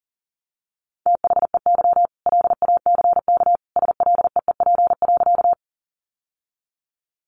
Morse code, heard as a keyed tone, about 40 wpm, 700 Hz. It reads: THEY LACK SLEEP.
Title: Sentences from Top 1100 Words 40wpm